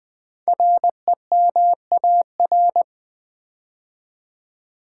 Morse code, heard as a keyed tone, about 20 wpm, 700 Hz. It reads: REMAR
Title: Build A Word - 5 Letters At A Time - 10-Letter Words - 20wpm